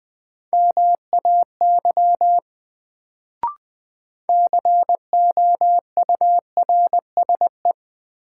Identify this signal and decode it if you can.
Morse code, heard as a keyed tone, about 20 wpm, 700 Hz. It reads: MAY COURSE
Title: Sets of 2 Words - New Word Drill - 20wpm